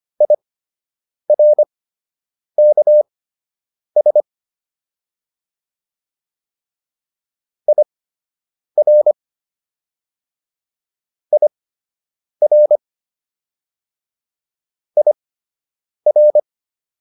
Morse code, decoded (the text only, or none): IRKS IR IR IR